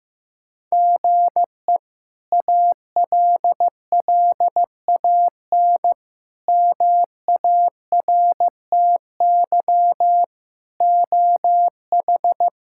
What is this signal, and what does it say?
Morse code, heard as a keyed tone, about 15 wpm, 700 Hz. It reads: GE ALLAN MARTY OH
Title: K1USN SST QSO Practice - 15wpm